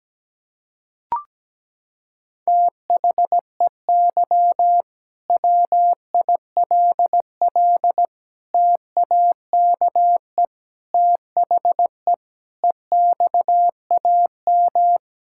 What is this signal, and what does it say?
Morse code, heard as a keyed tone, about 17 wpm, 700 Hz. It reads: THEY WILL TAKE THE EXAM
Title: Sentences from Top 1000 Words - Review Part 5 - 17wpm